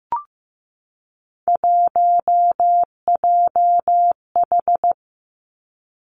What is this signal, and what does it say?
Morse code, heard as a keyed tone, about 15 wpm, 700 Hz. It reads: 1JH
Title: Three Challenging Letter-Number Combinations 15wpm